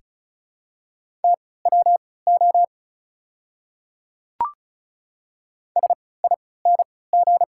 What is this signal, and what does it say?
Morse code, heard as a keyed tone, about 34 wpm, spaced as for 20 wpm, 700 Hz. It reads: TWO SING
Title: 3x Character Spacing - Top 500 Words - 35wpm